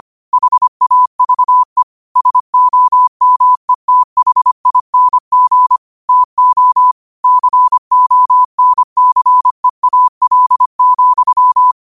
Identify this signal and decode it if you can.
Morse code, heard as a keyed tone, about 25 wpm, 1000 Hz. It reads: HAVE SOMETHING TO CONCEAL,
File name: qotdfiles / 25wpm